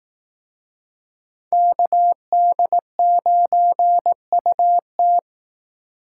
Morse code, heard as a keyed tone, about 18 wpm, 700 Hz. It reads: KD9UT
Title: Speed Racing - USA 2x2 Call Signs 15wpm